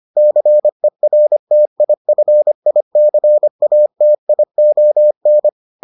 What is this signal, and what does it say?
Morse code, heard as a keyed tone, about 25 wpm, 600 Hz. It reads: CERTIFICATION